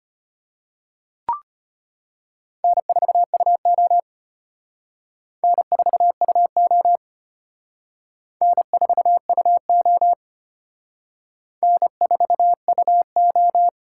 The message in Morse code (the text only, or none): N4UO N4UO N4UO N4UO